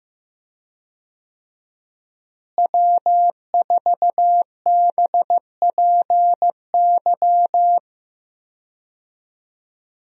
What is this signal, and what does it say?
Morse code, heard as a keyed tone, about 15 wpm, 700 Hz. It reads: W4BPY
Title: USA Call Signs - 5 Characters - Repeated 3x - 15wpm